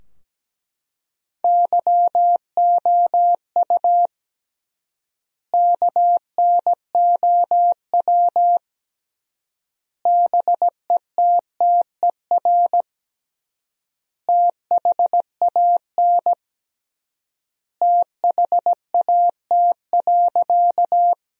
Morse code, heard as a keyed tone, about 17 wpm, 700 Hz. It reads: YOU KNOW BETTER THAN THAT.
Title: 3x Word-Spacing - Sentences from Top 400 Words - 17wpm